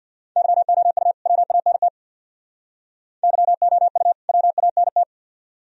Morse code, heard as a keyed tone, about 52 wpm, 700 Hz. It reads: YOU WANT YOU WANT
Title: Speed-Racing - Sets of 2 Words 50wpm